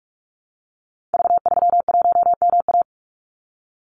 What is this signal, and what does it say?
Morse code, heard as a keyed tone, about 46 wpm, 700 Hz. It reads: V31MA
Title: Call Signs - 45wpm